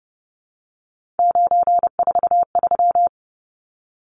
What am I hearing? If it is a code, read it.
Morse code, heard as a keyed tone, about 30 wpm, 700 Hz. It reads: 943
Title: Numbers - 3-Digit 30wpm